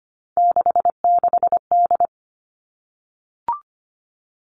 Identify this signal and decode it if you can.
Morse code, heard as a keyed tone, about 25 wpm, 700 Hz. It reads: 66D
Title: All Cut Numbers - 3-Digit 25wpm